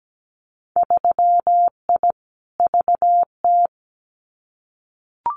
Morse code, heard as a keyed tone, about 17 wpm, 700 Hz. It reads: 3I VT
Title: Winter Field Day Exchanges Encore 17wpm